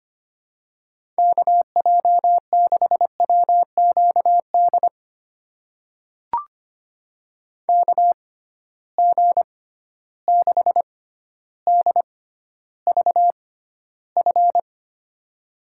Morse code, heard as a keyed tone, about 25 wpm, spaced as for 13 wpm, 700 Hz. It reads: KJ6WQD KG6DVF
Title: USA Call Signs - 6 Characters - 6x Character Spacing - 25wpm